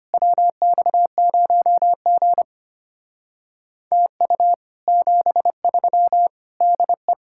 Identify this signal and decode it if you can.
Morse code, heard as a keyed tone, about 32 wpm, 700 Hz. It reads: WX0G TU 73 DE